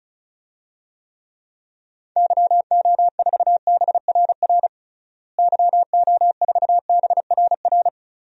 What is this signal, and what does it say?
Morse code, heard as a keyed tone, about 36 wpm, 700 Hz. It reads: YO4BRR YO4BRR
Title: DXCC-100 Call Signs - 6 Characters - Repeated 2x - 35wpm